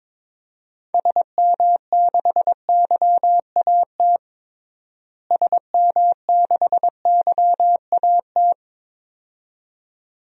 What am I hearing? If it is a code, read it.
Morse code, heard as a keyed tone, about 22 wpm, 700 Hz. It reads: SM6YAT SM6YAT
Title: DXCC-100 Call Signs - 6 Characters - Repeated 2x - 22wpm